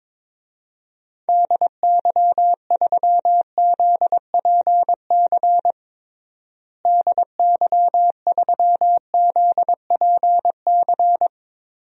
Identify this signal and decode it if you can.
Morse code, heard as a keyed tone, about 21 wpm, 700 Hz. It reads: DY3ZPC DY3ZPC
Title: DXCC-100 Call Signs - 6 Characters - Repeated 3x - 22wpm